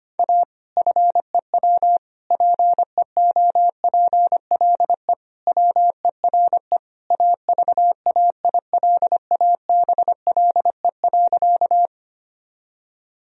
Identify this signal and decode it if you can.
Morse code, heard as a keyed tone, about 25 wpm, 700 Hz. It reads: A FEW PEOPLE WERE AVAILABLE.
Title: Sentences from Top 1000 Words - Review Part 3 - 25wpm